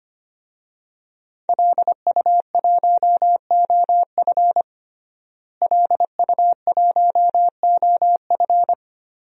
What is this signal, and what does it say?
Morse code, heard as a keyed tone, about 25 wpm, 700 Hz. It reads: LU1OF LU1OF
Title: DXCC-100 Call Signs - 5 Characters - Repeated 3x - 25wpm